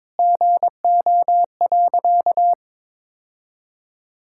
Morse code, heard as a keyed tone, about 22 wpm, 700 Hz. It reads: GO.
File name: practice / 22wpm / Sentences with 4-Word Phrases - 22wpm